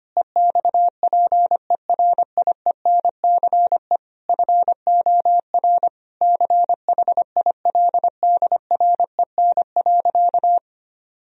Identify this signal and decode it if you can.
Morse code, heard as a keyed tone, about 25 wpm, 700 Hz. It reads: EXPERIENCE FOR CHILDREN.
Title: Sentences from Top 1000 Words - Review Part 4 - 25wpm